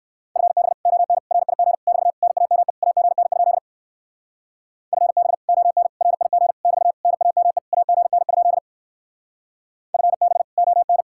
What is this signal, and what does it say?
Morse code, heard as a keyed tone, about 57 wpm, 700 Hz. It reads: FB ON RIG <BT> NAME AGN? FB ON RIG <BT> NAME AGN? FB ON